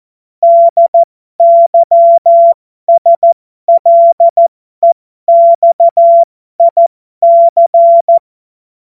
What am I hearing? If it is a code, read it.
Morse code, heard as a keyed tone, about 14 wpm, spaced as for 12 wpm, 680 Hz. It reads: DYSLEXIC
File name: Morsekurs / Witze